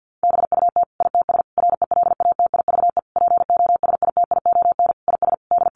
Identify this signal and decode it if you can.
Morse code, heard as a keyed tone, about 50 wpm, 700 Hz. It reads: BUT ITS RELATIVE POSITION IS D
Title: Sentences from Top 1400 Words 50wpm